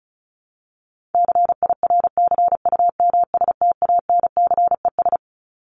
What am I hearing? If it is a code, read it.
Morse code, heard as a keyed tone, about 34 wpm, 700 Hz. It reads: CIRCUMSTANCES
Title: ES:S Suffix - Top 100 Words with Base Words 35wpm